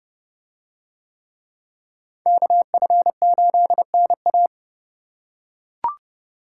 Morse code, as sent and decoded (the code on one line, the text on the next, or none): -.- ..-. ---.. -. .-
KF8NA